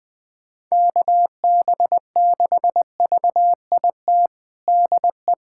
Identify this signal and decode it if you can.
Morse code, heard as a keyed tone, about 20 wpm, 700 Hz. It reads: KB6VIT DE
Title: USA Callsign DE Callsign - 6 Characters - 20wpm